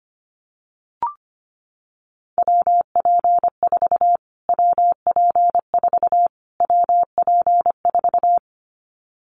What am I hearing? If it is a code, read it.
Morse code, heard as a keyed tone, about 25 wpm, 700 Hz. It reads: WP4 WP4 WP4
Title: Callsign Prefixes for Top 25 DXCC - Repeated 3x - 25wpm